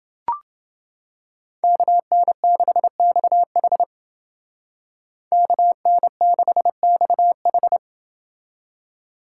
Morse code, heard as a keyed tone, about 29 wpm, 700 Hz. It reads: KN6XH KN6XH